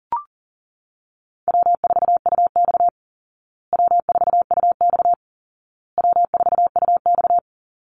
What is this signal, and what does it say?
Morse code, heard as a keyed tone, about 37 wpm, 700 Hz. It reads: W4UX W4UX W4UX
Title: Easy Call Signs Repeated 3x - 40wpm